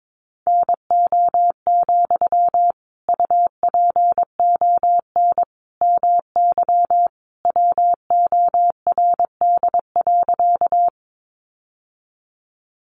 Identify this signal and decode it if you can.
Morse code, heard as a keyed tone, about 22 wpm, 700 Hz. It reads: NO, UPON MY WORD.
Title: Sentences from Top 1100 Words 22wpm